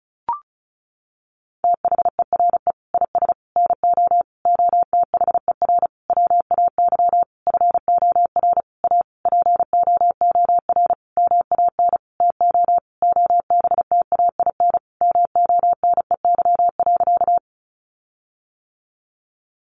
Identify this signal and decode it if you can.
Morse code, heard as a keyed tone, about 34 wpm, 700 Hz. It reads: THERE IS NO OTHER WAY FOR A POOR MAN TO OBTAIN MONEY.
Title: Sentences from Top 1300 Words 35wpm